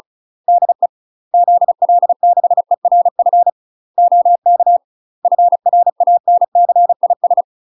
Morse code, heard as a keyed tone, about 34 wpm, 700 Hz. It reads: DE ZL6ERF OK FRANCIS
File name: UK MASTER QSOs TRAINING / 35WPM / UK MASTER QSOs 0000